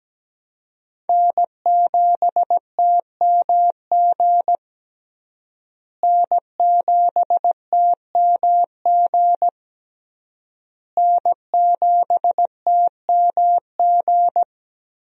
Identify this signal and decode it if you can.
Morse code, heard as a keyed tone, about 17 wpm, 700 Hz. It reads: N7TMG N7TMG N7TMG
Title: USA Call Signs - 5 Characters - Repeated 3x - 17wpm